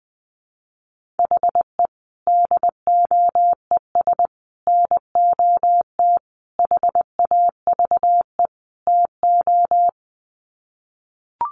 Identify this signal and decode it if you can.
Morse code, heard as a keyed tone, about 20 wpm, 700 Hz. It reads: HE DOES NOT HAVE TO
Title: Sets of 5 Words Encore 20wpm